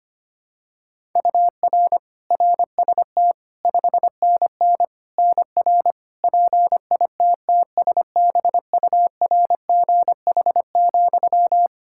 Morse code, heard as a keyed tone, about 25 wpm, 700 Hz. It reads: UR RST 5NN NR PITTSBURGH,
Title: Practice QSO - Exchange 1 - Beginning - 25wpm